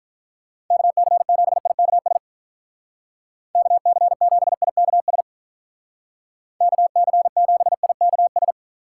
Morse code, heard as a keyed tone, about 43 wpm, 700 Hz. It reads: KC7IKS KC7IKS KC7IKS
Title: Speed Racing - USA 2x3 Call Signs 35wpm